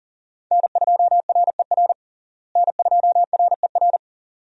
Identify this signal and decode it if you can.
Morse code, heard as a keyed tone, about 40 wpm, 700 Hz. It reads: N2RER N2RER
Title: USA Call Signs - 5 Characters - Repeated 2x - 40wpm